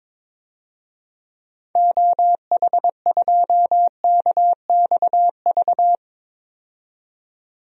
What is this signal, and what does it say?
Morse code, heard as a keyed tone, about 22 wpm, 700 Hz. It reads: OH2KXV
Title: DXCC-100 Call Signs - 6 Characters - 22wpm